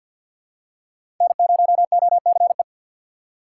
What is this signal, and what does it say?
Morse code, heard as a keyed tone, about 49 wpm, 700 Hz. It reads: N0OCE
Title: USA Call Signs - 4x Character Spacing - 50wpm